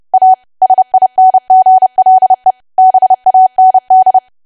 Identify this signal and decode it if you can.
Morse code, heard as a keyed tone, about 30 wpm, 750 Hz. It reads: A SINGLE BAND
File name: cwa / W1AW-Level3 / 30WPM